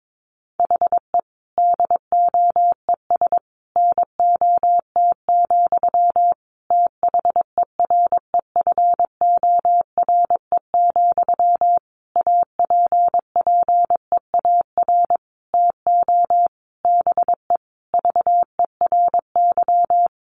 Morse code, heard as a keyed tone, about 22 wpm, 700 Hz. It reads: HE DOES NOT, THEREFORE, APPEAR TO BE VERY
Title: Sentences from Top 1200 Words 22wpm